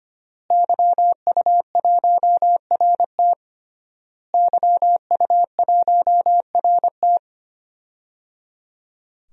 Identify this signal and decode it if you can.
Morse code, heard as a keyed tone, about 25 wpm, 700 Hz. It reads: YU1RT YU1RT
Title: DXCC-100 Call Signs - 5 Characters - Repeated 3x - 25wpm